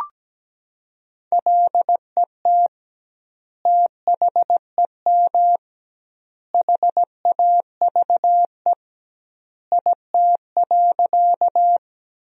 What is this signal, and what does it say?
Morse code, heard as a keyed tone, about 17 wpm, 700 Hz. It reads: LET THEM HAVE IT.
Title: 2x Word-Spacing - Sentences from Top 300 Words - 17wpm